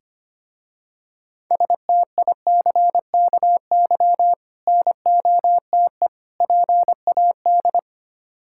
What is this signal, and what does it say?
Morse code, heard as a keyed tone, about 25 wpm, 700 Hz. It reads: STICKY NOTE PAD
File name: practice / 25wpm / Difficult Words with Hints - 25wpm